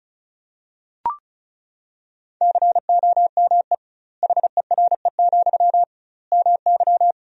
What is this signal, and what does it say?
Morse code, heard as a keyed tone, about 36 wpm, 700 Hz. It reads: COME HERE, MY
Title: 2x Word-Spacing - Sentences from Top 300 Words - 35wpm